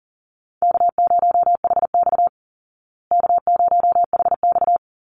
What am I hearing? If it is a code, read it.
Morse code, heard as a keyed tone, about 40 wpm, 700 Hz. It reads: K0HX K0HX
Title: Easy Call Signs Repeated 3x - 40wpm